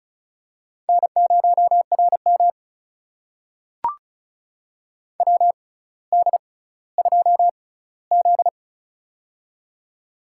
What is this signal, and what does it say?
Morse code, heard as a keyed tone, about 34 wpm, spaced as for 18 wpm, 700 Hz. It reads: N0RM WD2Z
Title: USA Call Signs - 4 Characters - 6x Character Spacing - 35wpm